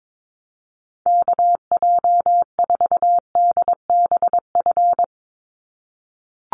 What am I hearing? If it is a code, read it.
Morse code, heard as a keyed tone, about 22 wpm, 700 Hz. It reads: KJ4DBF
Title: USA Call Signs 22wpm